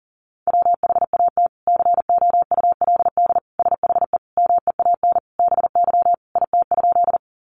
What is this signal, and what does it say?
Morse code, heard as a keyed tone, about 40 wpm, 700 Hz. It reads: WHAT COULD SHE MEAN BY IT?
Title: Sentences from Top 200 Words 40wpm